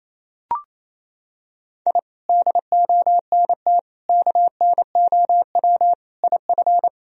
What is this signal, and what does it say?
Morse code, heard as a keyed tone, about 27 wpm, 700 Hz. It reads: I DONT KNOW IF
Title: Sentences from Top 1000 Words - Review Part 2 - 28wpm